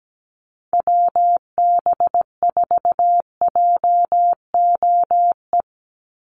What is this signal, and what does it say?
Morse code, heard as a keyed tone, about 17 wpm, 700 Hz. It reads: WB4JOE
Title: USA Call Signs 17wpm